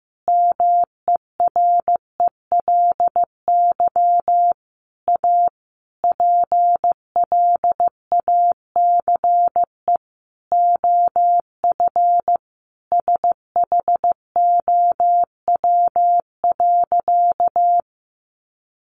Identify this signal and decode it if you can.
Morse code, heard as a keyed tone, about 14 wpm, 700 Hz. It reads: MERELY A PLACE OF SHOW.
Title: Sentences from Top 1400 Words 15wpm